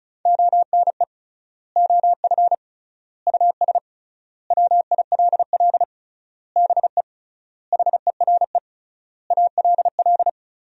Morse code, heard as a keyed tone, about 36 wpm, 700 Hz. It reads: ONE OF US WILL BE HERE ALL
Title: 3x Word-Spacing - Sentences from Top 300 Words - 35wpm